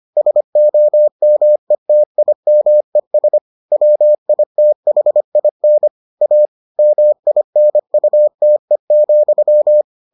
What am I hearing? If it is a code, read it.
Morse code, heard as a keyed tone, about 25 wpm, 600 Hz. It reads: SOMETIMES WITHIN A MINUTE,